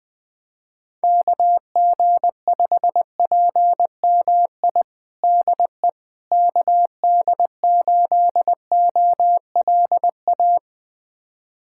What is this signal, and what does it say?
Morse code, heard as a keyed tone, about 20 wpm, 700 Hz. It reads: KG5PMI DE KD8OLA
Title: USA Callsign DE Callsign - 6 Characters - 20wpm